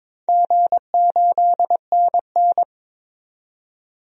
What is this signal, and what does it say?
Morse code, heard as a keyed tone, about 22 wpm, 700 Hz. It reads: G8NN
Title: DXCC-100 Call Signs - 4 Characters - 22wpm